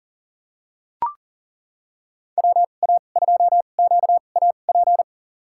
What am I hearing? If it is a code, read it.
Morse code, heard as a keyed tone, about 40 wpm, spaced as for 29 wpm, 700 Hz. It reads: WA2QAP
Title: USA Call Signs - 2x Character Spacing - 40wpm